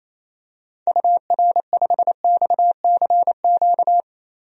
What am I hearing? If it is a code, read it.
Morse code, heard as a keyed tone, about 28 wpm, 700 Hz. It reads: UR5XCQ
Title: DXCC-100 Call Signs - 6 Characters - 28wpm